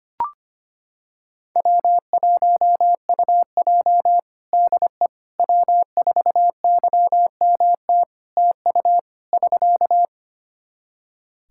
Morse code, heard as a keyed tone, about 25 wpm, 700 Hz. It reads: W1UJ DE W4YMT TU <SK>
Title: Practice QSO - Exchange 4 - Ending - 25wpm